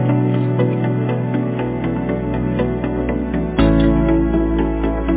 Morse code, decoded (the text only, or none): none